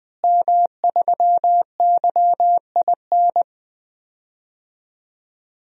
Morse code, heard as a keyed tone, about 20 wpm, 700 Hz. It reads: M3YIN
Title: DXCC-100 Call Signs - 5 Characters - Repeated 2x - 20wpm